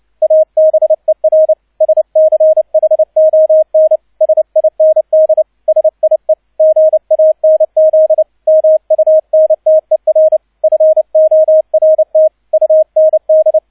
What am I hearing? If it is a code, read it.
Morse code, heard as a keyed tone, about 29 wpm, 630 Hz, with radio noise. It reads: ABER SCHON SIND SIE GANZ MUNTER FORT UND
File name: 002 Zweiter Streich 30wpm